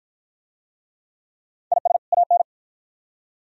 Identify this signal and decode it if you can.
Morse code, heard as a keyed tone, about 48 wpm, 700 Hz. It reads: IS AN